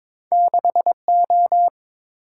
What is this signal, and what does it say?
Morse code, heard as a keyed tone, about 31 wpm, spaced as for 27 wpm, 700 Hz. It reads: 6O
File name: Speed-Racing - Two Letter-Number Combinations - 22wpm